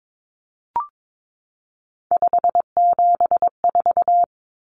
Morse code, heard as a keyed tone, about 22 wpm, 700 Hz. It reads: 574
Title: Numbers - 3-Digit 22wpm